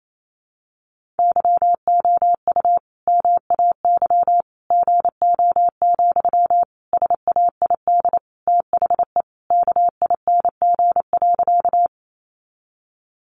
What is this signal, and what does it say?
Morse code, heard as a keyed tone, about 28 wpm, 700 Hz. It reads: YOU MAY GO, SAID THE KING.
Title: Sentences from Top 400 Words 28wpm